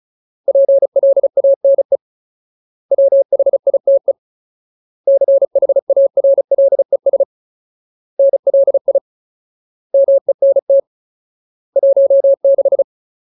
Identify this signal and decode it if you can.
Morse code, heard as a keyed tone, about 34 wpm, 550 Hz. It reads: PLANE WHITE CHARLES NLI MENT 16
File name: Ultimate Word List / 35WPM Ultimate